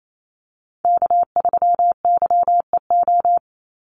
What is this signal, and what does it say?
Morse code, heard as a keyed tone, about 28 wpm, 700 Hz. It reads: K3YEO
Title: Call Signs Repeated 2x - 28wpm